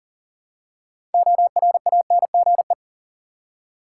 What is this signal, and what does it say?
Morse code, heard as a keyed tone, about 40 wpm, 700 Hz. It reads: ORANGE
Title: Six Letter Words - 40wpm